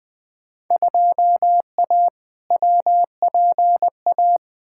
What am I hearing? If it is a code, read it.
Morse code, heard as a keyed tone, about 20 wpm, 700 Hz. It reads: 2A WPA
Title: Field Day Exchanges - 20wpm